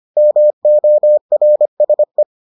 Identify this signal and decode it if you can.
Morse code, heard as a keyed tone, about 25 wpm, 600 Hz. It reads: MORSE